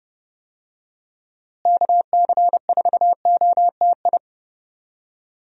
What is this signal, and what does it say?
Morse code, heard as a keyed tone, about 30 wpm, 700 Hz. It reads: KC4OTI
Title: USA Call Signs - 6 Characters - Repeated 2x - 30wpm